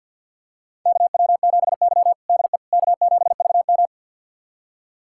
Morse code, heard as a keyed tone, about 53 wpm, 690 Hz. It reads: KW7Y DE K7VM
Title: USA Callsign DE Callsign - 4 Characters - 50wpm